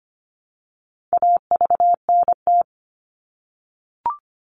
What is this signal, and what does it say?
Morse code, heard as a keyed tone, about 25 wpm, 700 Hz. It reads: AVNT
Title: All Cut Numbers - 4-Digit 25wpm